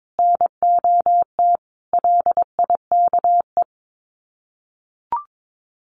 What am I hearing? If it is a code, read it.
Morse code, heard as a keyed tone, about 22 wpm, 700 Hz. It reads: NOT LIKE
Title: Sets of 5 Words 22wpm